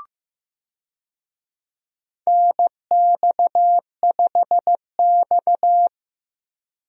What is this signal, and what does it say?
Morse code, heard as a keyed tone, about 15 wpm, 700 Hz. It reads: NX5X
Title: USA Call Signs - 4 Characters - 15wpm